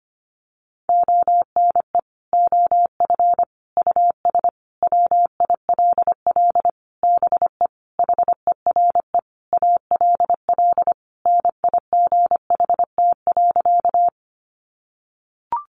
Morse code, heard as a keyed tone, about 25 wpm, 700 Hz. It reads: ONE OF US WILL BE HERE ALL NIGHT.
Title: Sentences from Top 300 Words 25wpm